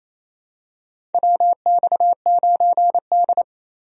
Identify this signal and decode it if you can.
Morse code, heard as a keyed tone, about 28 wpm, 700 Hz. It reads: WX9D
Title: USA Call Signs - 4 Characters - 28wpm